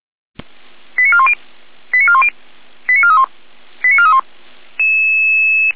Five-tone selective call (ZVEI), heard as quick runs of five tones, tones of about 70 ms. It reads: 89410 89410 89431 89431